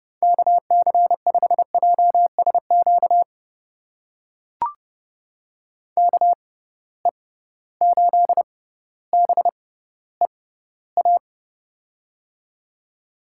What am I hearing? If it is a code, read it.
Morse code, heard as a keyed tone, about 32 wpm, spaced as for 16 wpm, 700 Hz. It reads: KC5JSQ KE8BEA
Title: USA Call Signs - 6 Characters - 6x Character Spacing - 30wpm